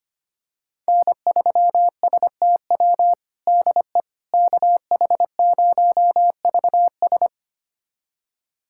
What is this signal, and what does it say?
Morse code, heard as a keyed tone, about 25 wpm, 700 Hz. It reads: N3STW DE KH0VS
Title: USA Callsign DE Callsign - 5 Characters - 25wpm